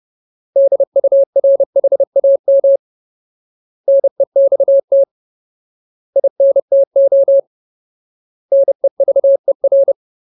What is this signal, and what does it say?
Morse code, heard as a keyed tone, about 30 wpm, 550 Hz. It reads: DURHAM NEXT INTO NEVER
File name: Ultimate Word List / 30WPM Ultimate